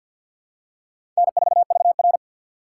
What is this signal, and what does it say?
Morse code, heard as a keyed tone, about 49 wpm, 700 Hz. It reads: N4VR
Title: USA Call Signs - 4 Characters - 2x Character Spacing - 50wpm